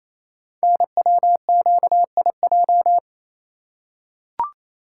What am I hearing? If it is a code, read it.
Morse code, heard as a keyed tone, about 29 wpm, 700 Hz. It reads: NWQIJ
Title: Code Groups - 5 Letter:Numbers - 28wpm